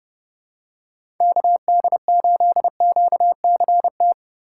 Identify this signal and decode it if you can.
Morse code, heard as a keyed tone, about 32 wpm, 700 Hz. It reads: KD8QCT
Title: USA Call Signs - 6 Characters - 3x Character Spacing - 30wpm